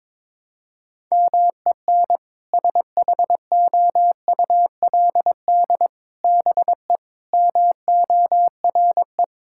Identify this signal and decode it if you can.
Morse code, heard as a keyed tone, about 22 wpm, 700 Hz. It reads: MEN SHOULD BE MORE
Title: Sentences from Top 1600 Words 22wpm